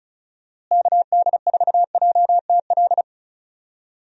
Morse code, heard as a keyed tone, about 36 wpm, 700 Hz. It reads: KD4JTL
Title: USA Call Signs - 6 Characters - Repeated 2x - 35wpm